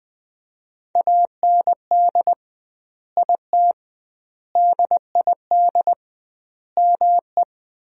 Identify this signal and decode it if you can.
Morse code, heard as a keyed tone, about 21 wpm, 700 Hz. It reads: AND IT DID ME
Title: 2x Word-Spacing - Sentences from Top 200 Words - 20wpm